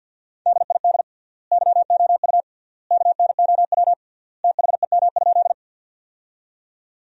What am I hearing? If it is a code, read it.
Morse code, heard as a keyed tone, about 50 wpm, 700 Hz. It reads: DID YOU KNOW THEM?
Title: 3x Word-Spacing - Sentences from Top 100 Words - 50wpm